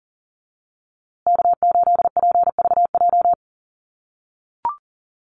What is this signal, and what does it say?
Morse code, heard as a keyed tone, about 40 wpm, 700 Hz. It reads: K8PVJ
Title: USA Call Signs Repeated 3x 40wpm